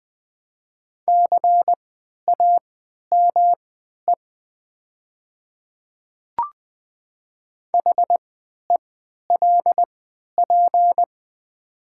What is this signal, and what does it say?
Morse code, heard as a keyed tone, about 20 wpm, spaced as for 11 wpm, 700 Hz. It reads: CAME HELP
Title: 3x Character Spacing - Top 300 Words - 20wpm